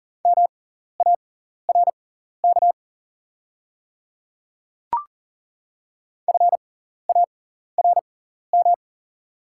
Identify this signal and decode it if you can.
Morse code, heard as a keyed tone, about 39 wpm, spaced as for 14 wpm, 700 Hz. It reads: MARK FARM
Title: 6x Character Spacing - Top 500 Words - 40wpm